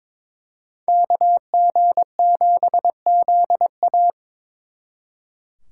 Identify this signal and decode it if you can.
Morse code, heard as a keyed tone, about 26 wpm, 700 Hz. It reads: KG7ZA
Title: Speed Racing - USA Call Signs 22wpm